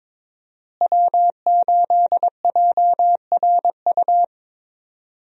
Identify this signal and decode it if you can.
Morse code, heard as a keyed tone, about 22 wpm, 700 Hz. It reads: W8JRU
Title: USA Call Signs - 5 Characters - Repeated 3x - 22wpm